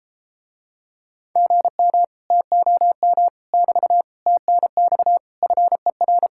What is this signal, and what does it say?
Morse code, heard as a keyed tone, about 33 wpm, 700 Hz. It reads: GM TOM <BT> TNX FER